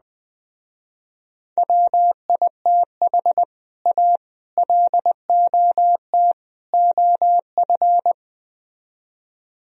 Morse code, heard as a keyed tone, about 20 wpm, 700 Hz. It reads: WITH A LOT OF
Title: Speed-Racing - Sets of 4 Words 17wpm